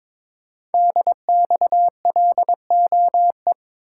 Morse code, heard as a keyed tone, about 22 wpm, 700 Hz. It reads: DXLOE
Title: Code Groups - 5 Letters - 22wpm